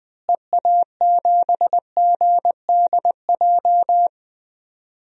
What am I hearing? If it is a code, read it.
Morse code, heard as a keyed tone, about 21 wpm, 700 Hz. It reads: EA7GDJ
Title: DXCC-100 Call Signs - 6 Characters - Repeated 3x - 20wpm